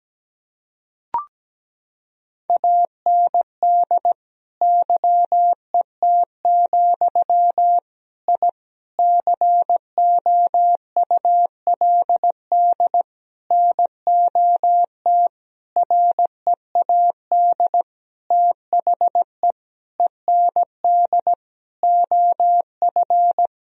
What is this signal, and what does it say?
Morse code, heard as a keyed tone, about 17 wpm, 700 Hz. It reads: AND YET, I COULD NOT READ THE END OF
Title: Sentences from Top 1000 Words - Review Part 1 - 17wpm